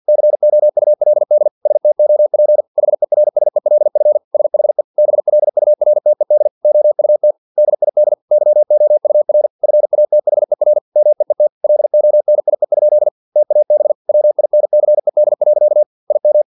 Morse code, heard as a keyed tone, about 50 wpm, 600 Hz. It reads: COULD STOP HERSELF SHE BLURTED OUT DID YOUR FATHER MEET LONIE? TAB WINCED. IM
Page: Never 49wpm